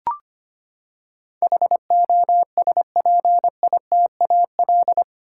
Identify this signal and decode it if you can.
Morse code, heard as a keyed tone, about 25 wpm, 700 Hz. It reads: HOSPITAL
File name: Eight Letter Words - 25wpm